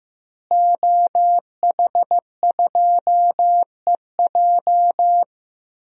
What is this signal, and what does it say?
Morse code, heard as a keyed tone, about 15 wpm, 700 Hz. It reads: OH2EJ
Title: DXCC-100 Call Signs - 5 Characters - Repeated 3x - 15wpm